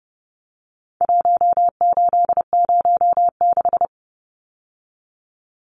Morse code, heard as a keyed tone, about 30 wpm, 700 Hz. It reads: 1806
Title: Numbers - 4-Digit 30wpm